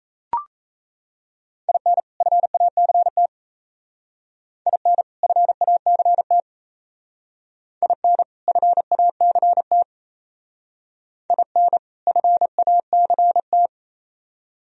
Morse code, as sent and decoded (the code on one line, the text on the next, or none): .. -. / ..-. .- -.-. - / .. -. / ..-. .- -.-. - / .. -. / ..-. .- -.-. - / .. -. / ..-. .- -.-. -
IN FACT IN FACT IN FACT IN FACT